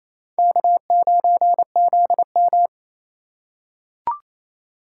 Morse code, heard as a keyed tone, about 28 wpm, 700 Hz. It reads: K9ZM
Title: USA Call Signs - 4 Characters - 4x Character Spacing - 28wpm